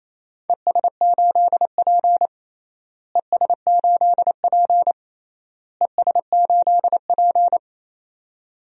Morse code, heard as a keyed tone, about 28 wpm, 700 Hz. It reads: ES8P ES8P ES8P
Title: DXCC-100 Call Signs - 4 Characters - Repeated 3x - 28wpm